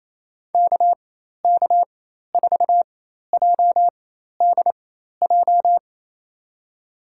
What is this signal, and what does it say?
Morse code, heard as a keyed tone, about 28 wpm, spaced as for 13 wpm, 700 Hz. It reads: KK4JDJ
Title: USA Call Signs - 4x Character Spacing - 28wpm